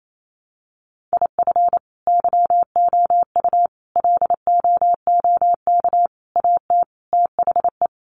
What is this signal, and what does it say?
Morse code, heard as a keyed tone, about 28 wpm, 700 Hz. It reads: IF YOU LOOK AT THE
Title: Sets of 5 Words 28wpm